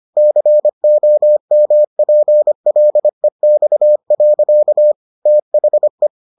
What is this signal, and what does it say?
Morse code, heard as a keyed tone, about 25 wpm, 600 Hz. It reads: COMPLEX. THE